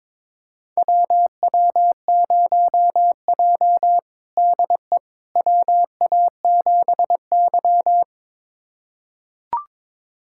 Morse code, heard as a keyed tone, about 23 wpm, 700 Hz. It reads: WW0J DE WA7Y
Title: USA Callsign DE Callsign - 4 Characters - 22wpm